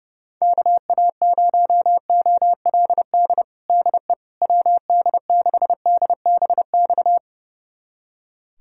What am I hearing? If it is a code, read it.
Morse code, heard as a keyed tone, about 32 wpm, 700 Hz. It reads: KA0OLD DE WD6DBX
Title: USA Callsign DE Callsign - 6 Characters - 30wpm